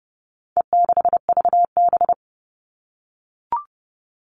Morse code, heard as a keyed tone, about 30 wpm, 700 Hz. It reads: E6VB
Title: All Cut Numbers - 4-Digit 30wpm